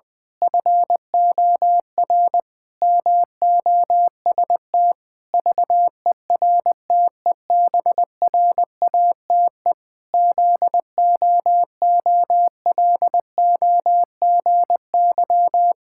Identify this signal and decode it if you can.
Morse code, heard as a keyed tone, about 20 wpm, 700 Hz. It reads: FOR MOST VERTEBRATE ZOOLOGY
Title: Pangram Challenge v2 - 20wpm